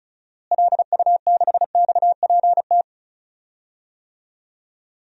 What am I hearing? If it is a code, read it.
Morse code, heard as a keyed tone, about 36 wpm, 700 Hz. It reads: LU6XPT
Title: DXCC-100 Call Signs - 6 Characters - Repeated 2x - 35wpm